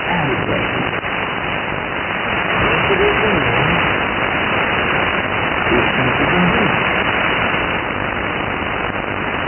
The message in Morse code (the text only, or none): none